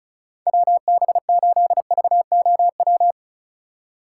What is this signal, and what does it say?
Morse code, heard as a keyed tone, about 36 wpm, 700 Hz. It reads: WB8VOW
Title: USA Call Signs - 6 Characters - 4x Character Spacing - 35wpm